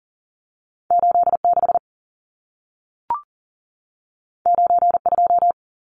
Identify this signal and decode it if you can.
Morse code, heard as a keyed tone, about 40 wpm, 700 Hz. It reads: 86 92
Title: Numbers - 2-Digit 40wpm